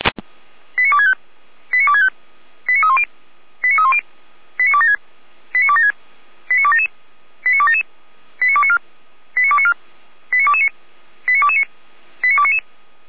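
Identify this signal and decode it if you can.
Five-tone selective call (ZVEI), heard as quick runs of five tones, tones of about 70 ms. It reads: 89265 89265 89310 89310 89276 89276 89277 89277 89294 89294 89229 89229 89299
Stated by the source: ZVEI five-tone selective call received on an SDR, from fire brigade Feuerwehr Orlamünde.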